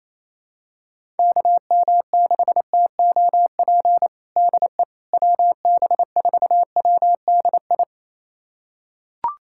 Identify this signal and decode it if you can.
Morse code, heard as a keyed tone, about 27 wpm, 700 Hz. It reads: KM6TOP DE WB4WDI
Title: USA Callsign DE Callsign - 6 Characters - 28wpm